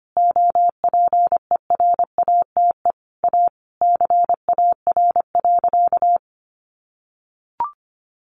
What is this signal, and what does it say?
Morse code, heard as a keyed tone, about 25 wpm, 700 Hz. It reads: OPERATE A CAR.
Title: Sentences from Top 1500 Words 25wpm